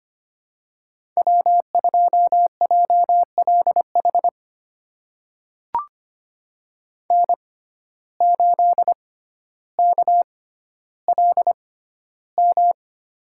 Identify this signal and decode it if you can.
Morse code, heard as a keyed tone, about 25 wpm, spaced as for 13 wpm, 700 Hz. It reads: W2JLH N8KLM
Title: USA Call Signs - 5 Characters - 6x Character Spacing - 25wpm